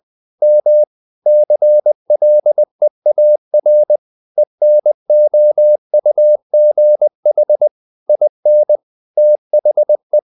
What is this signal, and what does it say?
Morse code, heard as a keyed tone, about 20 wpm, 600 Hz. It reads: M CLEAR ENOUGH IN THE
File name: the old man and the sea part 2 20wpm